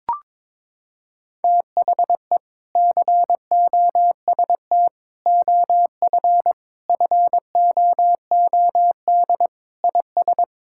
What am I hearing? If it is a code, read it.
Morse code, heard as a keyed tone, about 22 wpm, 700 Hz. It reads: THE COST OF FOOD IS
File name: practice / 22wpm / Sentences with 3-Word Phrases - 22wpm